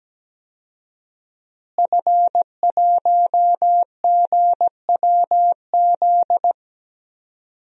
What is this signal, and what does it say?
Morse code, heard as a keyed tone, about 17 wpm, 700 Hz. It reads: F1GWZ
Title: DXCC-100 Call Signs - 5 Characters - Repeated 2x - 17wpm